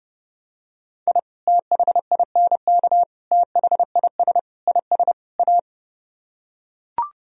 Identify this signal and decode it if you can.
Morse code, heard as a keyed tone, about 29 wpm, 700 Hz. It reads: I THINK THIS IS A